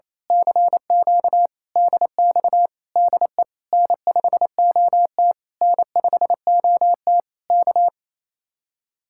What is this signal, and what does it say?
Morse code, heard as a keyed tone, about 26 wpm, 700 Hz. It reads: CQ DX DE N5OT N5OT K
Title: Speed-Racing Common QSO Phrases - 28wpm